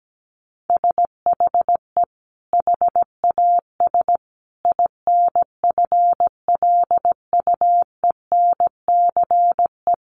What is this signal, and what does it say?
Morse code, heard as a keyed tone, about 17 wpm, 700 Hz. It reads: SHE HAS INFLUENCE
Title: Sentences from Top 1100 Words 17wpm